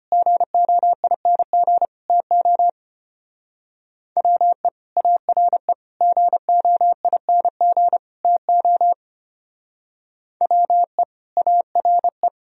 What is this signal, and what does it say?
Morse code, heard as a keyed tone, about 29 wpm, 700 Hz. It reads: GOING TO WE ARE GOING TO WE ARE
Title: Speed-Racing - Sets of 4 Words 25wpm